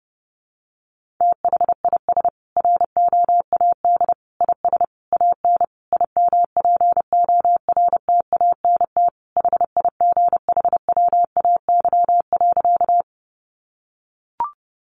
Morse code, heard as a keyed tone, about 30 wpm, 700 Hz. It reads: THIS ROAD IS AN IMPORTANT HIGHWAY.
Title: Sentences from Top 1400 Words 30wpm